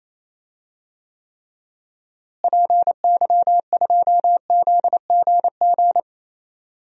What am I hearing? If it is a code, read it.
Morse code, heard as a keyed tone, about 28 wpm, 700 Hz. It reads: PY2ZGG
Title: DXCC-100 Call Signs - 6 Characters - Repeated 2x - 28wpm